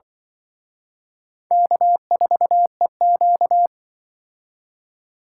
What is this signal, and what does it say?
Morse code, heard as a keyed tone, about 24 wpm, 700 Hz. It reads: K4EQ
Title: Speed Racing - USA Call Signs 20wpm